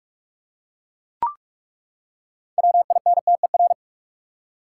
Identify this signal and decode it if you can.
Morse code, heard as a keyed tone, about 42 wpm, 700 Hz. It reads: WINTER
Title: Six Letter Words - 45wpm